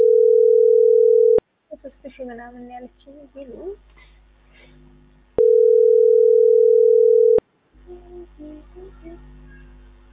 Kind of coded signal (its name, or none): telephone call progress tone